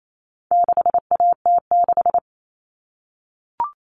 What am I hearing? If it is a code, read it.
Morse code, heard as a keyed tone, about 28 wpm, 700 Hz. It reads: 6AT6
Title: All Cut Numbers - 4-Digit 28wpm